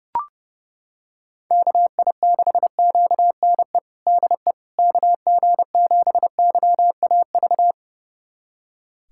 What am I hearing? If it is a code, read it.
Morse code, heard as a keyed tone, about 32 wpm, 700 Hz. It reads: KI6QNE DE KG7YAV
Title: USA Callsign DE Callsign 30wpm